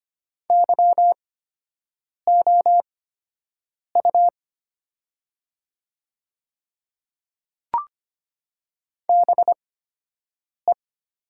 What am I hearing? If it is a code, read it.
Morse code, heard as a keyed tone, about 25 wpm, spaced as for 7 wpm, 700 Hz. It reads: YOU BE
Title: 8x Character Spacing - Top 100 Words - 25wpm